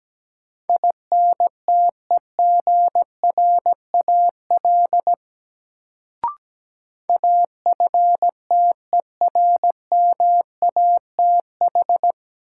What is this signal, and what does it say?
Morse code, heard as a keyed tone, about 17 wpm, 700 Hz. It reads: INTEGRAL AFTERMATH
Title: Sets of 5 Words - New Word Drill - 17wpm